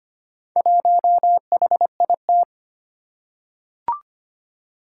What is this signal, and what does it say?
Morse code, heard as a keyed tone, about 25 wpm, 700 Hz. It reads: 1HIT
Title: Four Challenging Letter-Number Combinations 25wpm